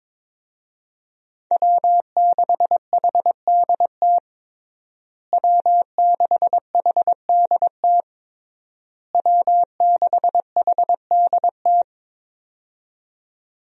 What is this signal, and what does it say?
Morse code, heard as a keyed tone, about 22 wpm, 700 Hz. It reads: W6HDT W6HDT W6HDT
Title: USA Call Signs - 5 Characters - Repeated 3x - 22wpm